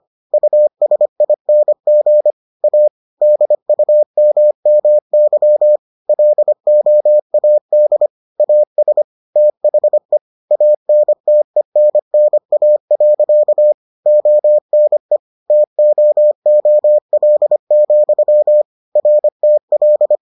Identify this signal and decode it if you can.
Morse code, heard as a keyed tone, about 25 wpm, 600 Hz. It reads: USING A DUMMY LOAD AS THE ANTENNA. ONE TOOL, RTL